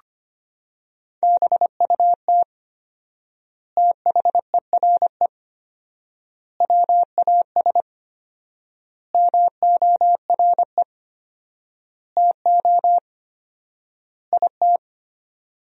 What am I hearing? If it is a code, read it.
Morse code, heard as a keyed tone, about 25 wpm, 700 Hz. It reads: BUT THERE WAS MORE TO IT
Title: 4x Word-Spacing - Sentences from Top 100 Words - 25wpm